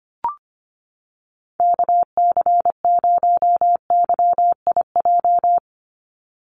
Morse code, heard as a keyed tone, about 25 wpm, 700 Hz. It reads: KC0YIJ
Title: USA Call Signs 25wpm